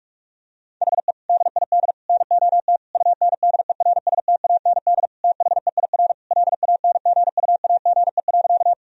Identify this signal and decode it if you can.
Morse code, heard as a keyed tone, about 46 wpm, 700 Hz. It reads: HE DID NOT UNDERSTAND THEIR LANGUAGE.
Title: Sentences from Top 1000 Words - Review Part 3 - 45wpm